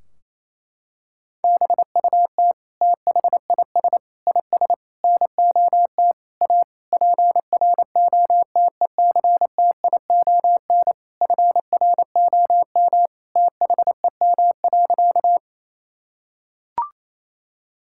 Morse code, heard as a keyed tone, about 28 wpm, 700 Hz. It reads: BUT THIS IS NOT A PROTECTION FROM THEM.
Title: Sentences from Top 1600 Words 28wpm